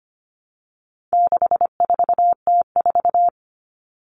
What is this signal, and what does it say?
Morse code, heard as a keyed tone, about 25 wpm, 700 Hz. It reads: 64T4
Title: All Cut Numbers - 4-Digit 25wpm